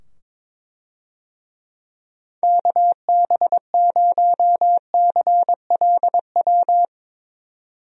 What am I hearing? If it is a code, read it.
Morse code, heard as a keyed tone, about 22 wpm, 700 Hz. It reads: KB0CLW
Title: USA Call Signs - 6 Characters - 22wpm